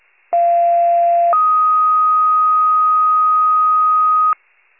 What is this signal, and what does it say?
Two-tone page, heard as a one-second tone, then a three-second tone, about 690 then 1250 Hz. It is heard from 0.3 to 4.3 s.